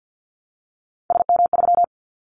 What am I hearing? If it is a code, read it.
Morse code, heard as a keyed tone, about 50 wpm, 700 Hz. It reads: SM3
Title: Callsign Prefixes for Top 25 DXCC - Repeated 3x - 50wpm